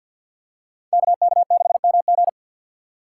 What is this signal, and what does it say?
Morse code, heard as a keyed tone, about 47 wpm, 700 Hz. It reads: KK6MG
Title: USA Call Signs - 5 Characters - 3x Character Spacing - 50wpm